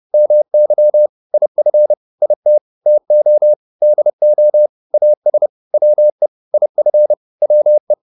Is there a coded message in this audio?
Yes: Morse code from 0.1 to 7.9 s, heard as a keyed tone, about 30 wpm, 600 Hz. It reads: MY IF IT TO DO AS WE IF WE